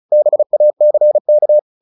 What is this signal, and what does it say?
Morse code, heard as a keyed tone, about 34 wpm, 600 Hz. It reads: BACK
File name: the old man and the sea 2 35wpm